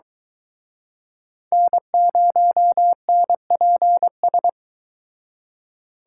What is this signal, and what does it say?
Morse code, heard as a keyed tone, about 23 wpm, 700 Hz. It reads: N0NPS